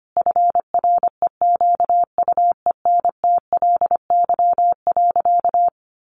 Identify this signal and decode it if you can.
Morse code, heard as a keyed tone, about 25 wpm, 700 Hz. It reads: FREQUENTLY.
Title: Sentences from Top 1300 Words 25wpm